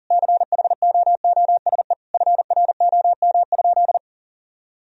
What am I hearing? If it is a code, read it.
Morse code, heard as a keyed tone, about 40 wpm, 700 Hz. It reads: CHOOSE FROM?
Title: Sentences from Top 1000 Words - Review Part 4 - 40wpm